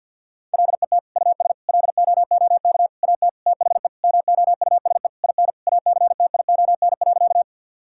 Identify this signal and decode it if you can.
Morse code, heard as a keyed tone, about 50 wpm, 700 Hz. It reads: LET US LOOK AT THE MOUSE IN ACTION.
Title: Sentences from Top 1000 Words - Review Part 5 - 50wpm